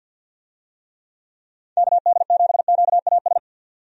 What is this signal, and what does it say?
Morse code, heard as a keyed tone, about 46 wpm, 700 Hz. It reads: KD7QAS
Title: USA Call Signs - 6 Characters - 50wpm